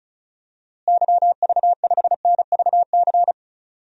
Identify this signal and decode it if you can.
Morse code, heard as a keyed tone, about 34 wpm, 700 Hz. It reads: YV5NVC
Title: DXCC-100 Call Signs - 6 Characters - Repeated 3x - 35wpm